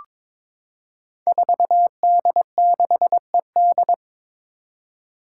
Morse code, heard as a keyed tone, about 22 wpm, 700 Hz. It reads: 4D6ED
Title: Five Challenging Letter-Number Combinations 22wpm